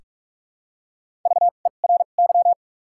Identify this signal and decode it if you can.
Morse code, heard as a keyed tone, about 42 wpm, spaced as for 31 wpm, 700 Hz. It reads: VERY
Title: 2x Character Spacing - Top 300 Words - 45wpm